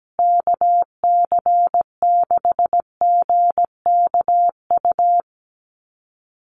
Morse code, heard as a keyed tone, about 17 wpm, 700 Hz. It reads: KC6GKU
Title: USA Call Signs Repeated 3x 17wpm